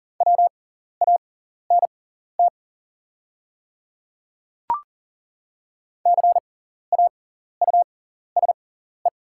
Morse code, heard as a keyed tone, about 40 wpm, spaced as for 14 wpm, 700 Hz. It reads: WANT CAUSE
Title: 6x Character Spacing - Top 200 Words - 40wpm